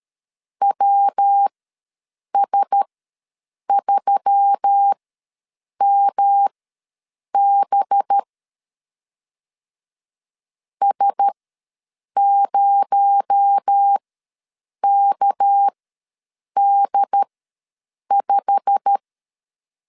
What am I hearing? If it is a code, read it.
Morse code, heard as a keyed tone, about 13 wpm, spaced as for 7 wpm, 800 Hz. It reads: WS3MB S0KD5
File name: LEZIONI / 8 YJ45 / 8 YJ45 ws 12wpm 3146